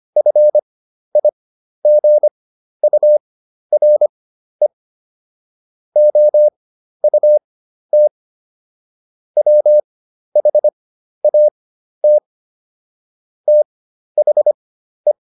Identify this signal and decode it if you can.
Morse code, heard as a keyed tone, about 25 wpm, spaced as for 12 wpm, 600 Hz. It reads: FIGURE OUT WHAT THE